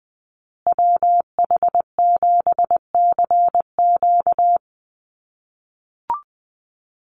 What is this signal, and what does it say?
Morse code, heard as a keyed tone, about 20 wpm, 700 Hz. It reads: WH7CQ
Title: USA Call Signs Repeated 2x 20wpm